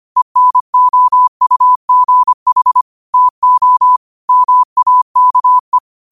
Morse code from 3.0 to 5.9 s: TO MAKE

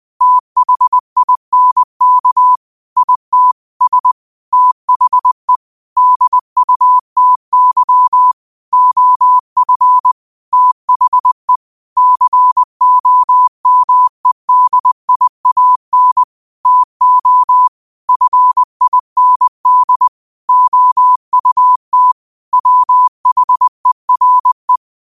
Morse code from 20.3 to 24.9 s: OUT WHERE

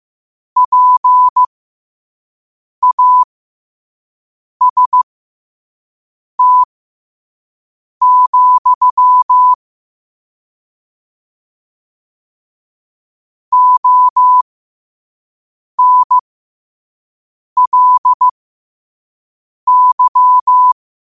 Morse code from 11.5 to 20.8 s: ONLY

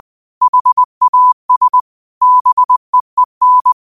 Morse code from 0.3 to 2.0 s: HAS

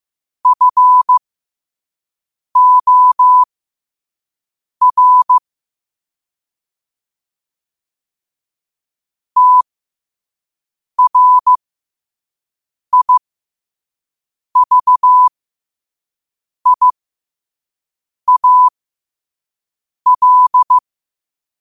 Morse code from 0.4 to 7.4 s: FOR

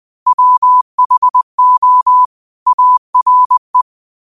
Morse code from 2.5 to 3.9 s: ARE

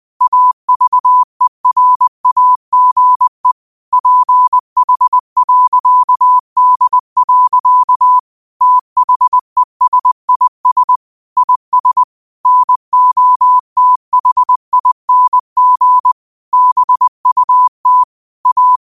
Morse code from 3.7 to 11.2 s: PH.D. THESIS

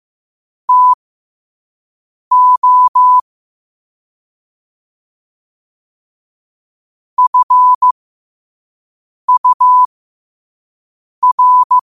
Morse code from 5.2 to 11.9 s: FUR

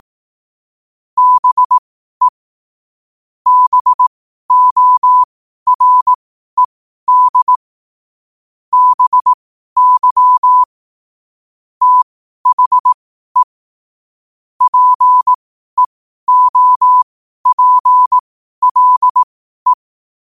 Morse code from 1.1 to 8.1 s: BE BORED

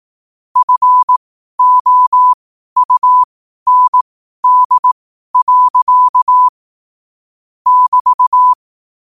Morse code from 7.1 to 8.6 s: <BT>

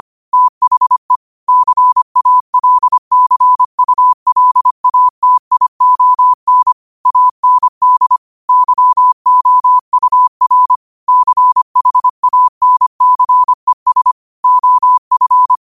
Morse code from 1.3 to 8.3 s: CALCULATION AND

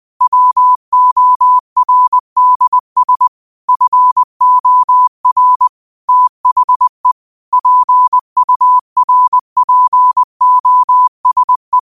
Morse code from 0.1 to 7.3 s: WORDS FOR THE

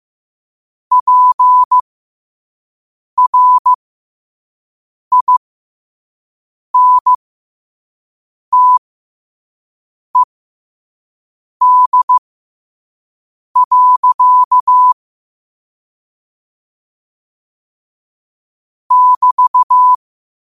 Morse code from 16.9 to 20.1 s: <BT>

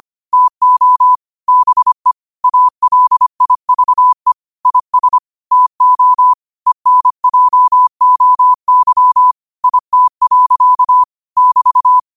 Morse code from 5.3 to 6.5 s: TO

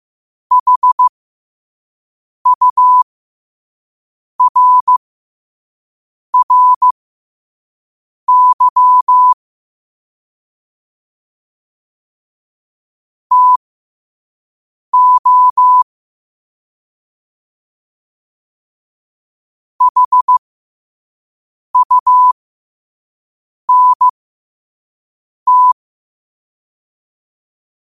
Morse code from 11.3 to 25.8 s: TO HUNT

